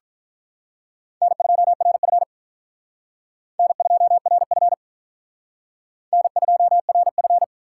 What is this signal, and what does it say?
Morse code, heard as a keyed tone, about 45 wpm, 700 Hz. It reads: N2RF N2RF N2RF